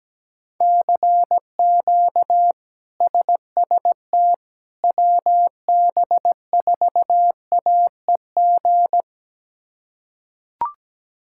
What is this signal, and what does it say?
Morse code, heard as a keyed tone, about 17 wpm, 700 Hz. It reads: CQ SST WB4AEG
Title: K1USN SST QSO Practice - 17wpm